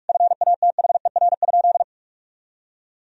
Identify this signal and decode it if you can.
Morse code, heard as a keyed tone, about 41 wpm, 700 Hz. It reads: FATHER?
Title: Sentences from Top 1000 Words - Review Part 3 - 45wpm